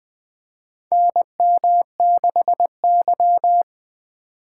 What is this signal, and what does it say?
Morse code, heard as a keyed tone, about 20 wpm, 700 Hz. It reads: NM6Y
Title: USA Call Signs - 4 Characters - 3x Character Spacing - 20wpm